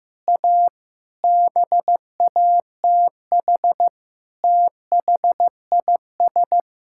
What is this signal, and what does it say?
Morse code, heard as a keyed tone, about 15 wpm, 700 Hz. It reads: A BATH THIS
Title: Sentences from Top 1600 Words 15wpm